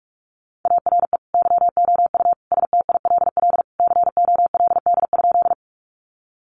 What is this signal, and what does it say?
Morse code, heard as a keyed tone, about 44 wpm, 700 Hz. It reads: ARE YOU STILL COLD?
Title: Sentences from Top 400 Words 45wpm